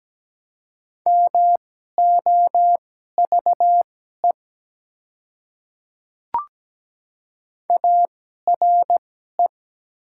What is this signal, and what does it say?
Morse code, heard as a keyed tone, about 17 wpm, spaced as for 12 wpm, 700 Hz. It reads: MOVE ARE
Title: 2x Character Spacing - Top 200 Words - 17wpm